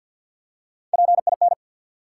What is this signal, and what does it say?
Morse code, heard as a keyed tone, about 51 wpm, 700 Hz. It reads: PIN
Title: Three Letter Words - 50wpm